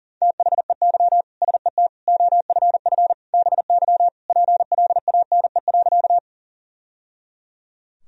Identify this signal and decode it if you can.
Morse code, heard as a keyed tone, about 40 wpm, 700 Hz. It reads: THEY SET OFF BY PLANE.
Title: Sentences from Top 1000 Words - Review Part 3 - 40wpm